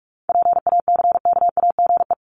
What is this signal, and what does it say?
Morse code, heard as a keyed tone, about 44 wpm, 700 Hz. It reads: PACKAGE
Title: Top 1501-1750 Words 45wpm